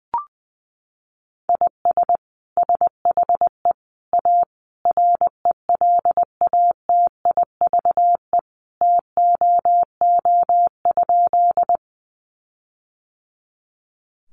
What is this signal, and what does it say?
Morse code, heard as a keyed tone, about 20 wpm, 700 Hz. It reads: IS SHE A RELATIVE TOO?
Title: Sentences from Top 1400 Words 20wpm